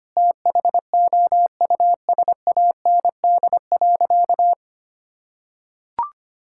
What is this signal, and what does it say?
Morse code, heard as a keyed tone, about 25 wpm, 700 Hz. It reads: THOUSAND.
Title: Sentences from Top 1000 Words - Review Part 5 - 25wpm